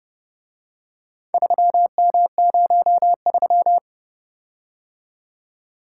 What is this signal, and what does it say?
Morse code, heard as a keyed tone, about 29 wpm, 700 Hz. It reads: 3M03